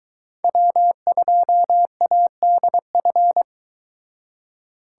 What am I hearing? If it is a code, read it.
Morse code, heard as a keyed tone, about 23 wpm, 700 Hz. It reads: W2ADF